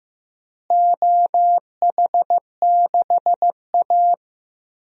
Morse code, heard as a keyed tone, about 15 wpm, 700 Hz. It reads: OH6A
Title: DXCC-100 Call Signs - 4 Characters - Repeated 2x - 15wpm